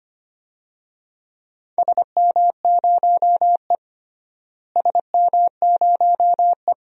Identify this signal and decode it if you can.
Morse code, heard as a keyed tone, about 25 wpm, 700 Hz. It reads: SM0E SM0E
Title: DXCC-100 Call Signs - 4 Characters - Repeated 3x - 25wpm